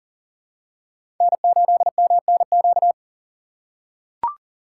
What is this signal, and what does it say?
Morse code, heard as a keyed tone, about 40 wpm, 700 Hz. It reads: N8MNQ
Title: USA Call Signs - 5 Characters - 2x Character Spacing - 40wpm